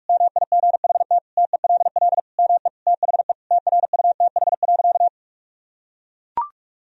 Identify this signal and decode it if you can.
Morse code, heard as a keyed tone, about 42 wpm, 700 Hz. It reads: MIGHT TELL ME THE TRUTH.
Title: Sentences from Top 1000 Words - Review Part 1 - 45wpm